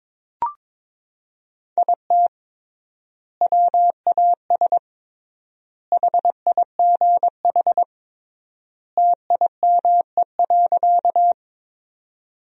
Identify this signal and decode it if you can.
Morse code, heard as a keyed tone, about 22 wpm, 700 Hz. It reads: IT WAS HIGH TIME.
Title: 3x Word-Spacing - Sentences from Top 200 Words - 22wpm